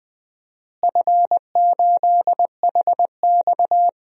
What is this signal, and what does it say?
Morse code, heard as a keyed tone, about 20 wpm, 700 Hz. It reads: F8HX
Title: DXCC-100 Call Signs - 4 Characters - Repeated 2x - 20wpm